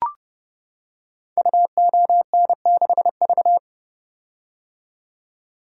Morse code, heard as a keyed tone, about 30 wpm, 700 Hz. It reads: UON6V
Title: Code Groups - 5 Letter:Numbers - 30wpm